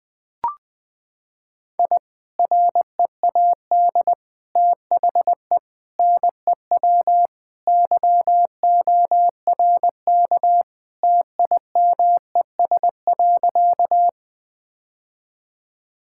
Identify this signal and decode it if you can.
Morse code, heard as a keyed tone, about 20 wpm, 700 Hz. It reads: I READ THE NEW YORK TIMES.
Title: Sentences with 3-Word Phrases - 20wpm